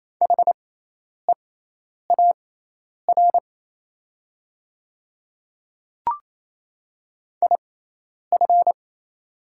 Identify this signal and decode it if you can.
Morse code, heard as a keyed tone, about 27 wpm, spaced as for 10 wpm, 700 Hz. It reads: HEAR IF
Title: 6x Character Spacing - Top 300 Words - 28wpm